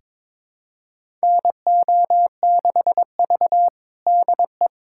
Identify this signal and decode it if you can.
Morse code, heard as a keyed tone, about 21 wpm, 700 Hz. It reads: NO6V DE
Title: USA Callsign DE Callsign - 4 Characters - 22wpm